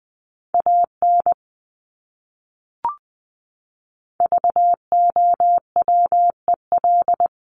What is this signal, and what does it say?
Morse code, heard as a keyed tone, about 20 wpm, 700 Hz. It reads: AN VOWEL
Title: Top 500 Words Repeated 20wpm